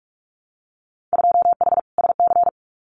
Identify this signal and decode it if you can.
Morse code, heard as a keyed tone, about 46 wpm, 700 Hz. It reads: 2H SC
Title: Winter Field Day Exchanges 45wpm